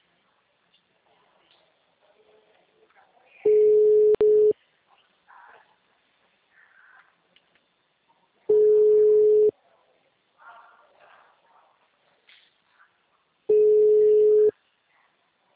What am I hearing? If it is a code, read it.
Ringback tone (Europe, 425 Hz).